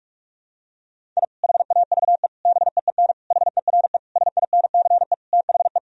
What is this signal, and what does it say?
Morse code, heard as a keyed tone, about 46 wpm, 700 Hz. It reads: I HAVE BEEN HERE SINCE THE
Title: Sentences from Top 1000 Words - Review Part 1 - 45wpm